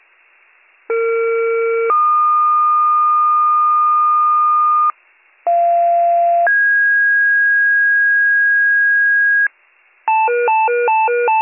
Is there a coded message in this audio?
Yes: two-tone page from 0.9 to 9.5 s, heard as a one-second tone, then a three-second tone, about 460 then 1220 Hz, about 690 then 1650 Hz, with noise.